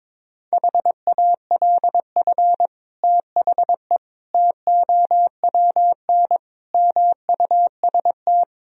Morse code, heard as a keyed tone, about 22 wpm, 700 Hz. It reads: HALF THE TOWN MUST
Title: Sentences from Top 1000 Words - Review Part 2 - 22wpm